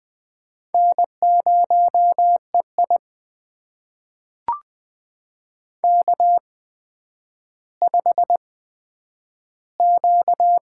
Morse code, heard as a keyed tone, about 20 wpm, spaced as for 10 wpm, 700 Hz. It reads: N0EI K5Q
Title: USA Call Signs - 4 Characters - 8x Character Spacing - 20wpm